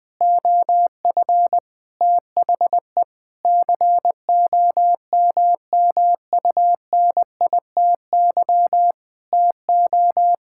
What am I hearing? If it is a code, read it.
Morse code, heard as a keyed tone, about 19 wpm, 700 Hz. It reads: OF THE COMMUNITY TO
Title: Sentences with 7-Word Phrases - 20wpm